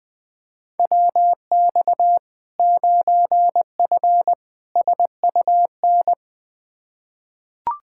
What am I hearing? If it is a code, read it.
Morse code, heard as a keyed tone, about 20 wpm, 700 Hz. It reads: WX 9F SUN
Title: Practice QSO - Exchange 2 - Info - 20wpm